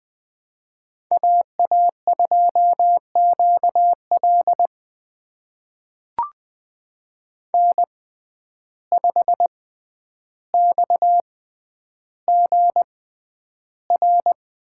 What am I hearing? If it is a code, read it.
Morse code, heard as a keyed tone, about 20 wpm, spaced as for 10 wpm, 700 Hz. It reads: AA2QL N5XGR
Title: USA Call Signs - 5 Characters - 6x Character Spacing - 20wpm